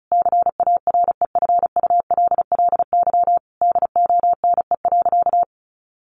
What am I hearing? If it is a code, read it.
Morse code, heard as a keyed tone, about 34 wpm, 700 Hz. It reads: CAREFULLY DONE.
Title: Sentences from Top 1300 Words 35wpm